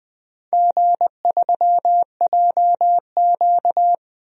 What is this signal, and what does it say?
Morse code, heard as a keyed tone, about 20 wpm, 700 Hz. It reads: G3JQ
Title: DXCC-100 Call Signs - 4 Characters - 20wpm